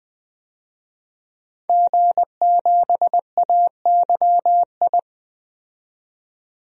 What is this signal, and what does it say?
Morse code, heard as a keyed tone, about 21 wpm, 700 Hz. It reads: G7AYI
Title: DXCC-100 Call Signs - 5 Characters - 20wpm